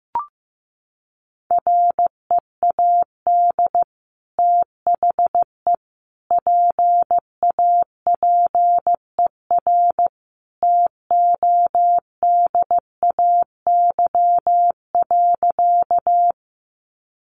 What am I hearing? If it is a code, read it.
Morse code, heard as a keyed tone, about 15 wpm, 700 Hz. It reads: READ THE PAPER TODAY.
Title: Sentences from Top 600 Words 15wpm